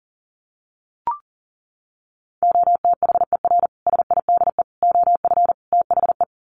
Morse code, heard as a keyed tone, about 40 wpm, 700 Hz. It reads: OTHER SIDE OF THE
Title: Sets of 4 Words 40wpm